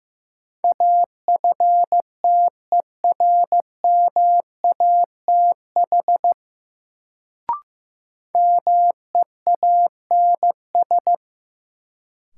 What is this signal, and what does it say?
Morse code, heard as a keyed tone, about 15 wpm, 700 Hz. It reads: AFTERMATH MEANS
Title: Sets of 5 Words - New Word Drill - 15wpm